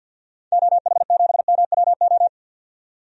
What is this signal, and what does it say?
Morse code, heard as a keyed tone, about 53 wpm, 700 Hz. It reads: OH7MWO
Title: DXCC-100 Call Signs - 6 Characters - 50wpm